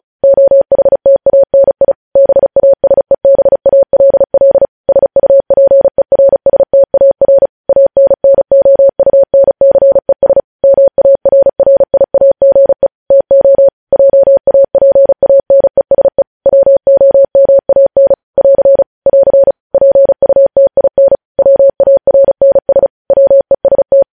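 Morse code, heard as a keyed tone, about 36 wpm, 550 Hz. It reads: OHTANI BASEBALL SUPERSTAR ANNOUNCES MARRIAGE TO JAPANESE WOMAN <AR> <AR> PUTIN WARNS WEST
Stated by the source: News headlines sent in Morse at speed 35.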